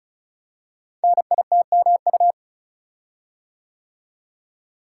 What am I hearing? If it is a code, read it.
Morse code, heard as a keyed tone, about 36 wpm, 700 Hz. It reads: NITMU
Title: Code Groups - 5 Letter:Numbers - 35wpm